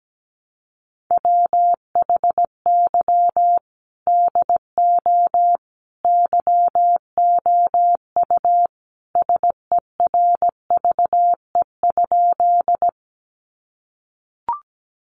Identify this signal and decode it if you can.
Morse code, heard as a keyed tone, about 17 wpm, 700 Hz. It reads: WHY DO YOU SERVE?
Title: Sentences from Top 400 Words 17wpm